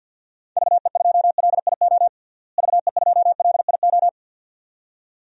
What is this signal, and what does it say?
Morse code, heard as a keyed tone, about 49 wpm, 700 Hz. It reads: VE2LIO VE2LIO
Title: DXCC-100 Call Signs - 6 Characters - Repeated 2x - 50wpm